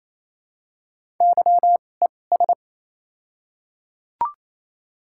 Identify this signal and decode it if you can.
Morse code, heard as a keyed tone, about 28 wpm, spaced as for 20 wpm, 700 Hz. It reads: YES
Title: 2x Character Spacing - Top 500 Words - 28wpm